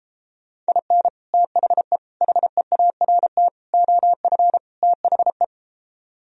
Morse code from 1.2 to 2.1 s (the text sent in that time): THE